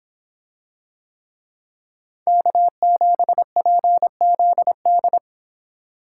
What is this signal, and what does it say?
Morse code, heard as a keyed tone, about 25 wpm, 700 Hz. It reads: K7PZD